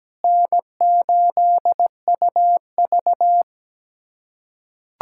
Morse code, heard as a keyed tone, about 17 wpm, 700 Hz. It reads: N8UV
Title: USA Callsign DE Callsign - 4 Characters - 17wpm